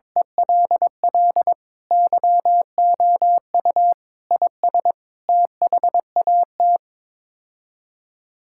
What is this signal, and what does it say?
Morse code, heard as a keyed tone, about 22 wpm, 700 Hz. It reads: ELL YOU IS THAT
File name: practice / 22wpm / Sets of 7 Words - 22wpm